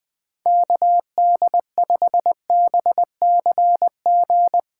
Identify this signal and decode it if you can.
Morse code, heard as a keyed tone, about 21 wpm, 700 Hz. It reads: KD5BCG
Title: USA Callsign DE Callsign 20wpm